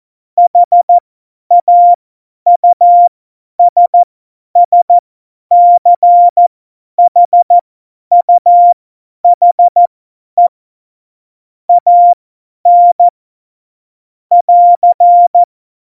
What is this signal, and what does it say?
Morse code, heard as a keyed tone, about 14 wpm, spaced as for 10 wpm, 700 Hz. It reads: HAUSSCHUHE AN <AR>
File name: Morsekurs / Witze / Witz 15. Mai 2018, 50 F70